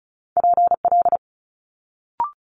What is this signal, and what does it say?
Morse code, heard as a keyed tone, about 36 wpm, 700 Hz. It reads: PL
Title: Multi-letter Phonemes Quick Fire 35wpm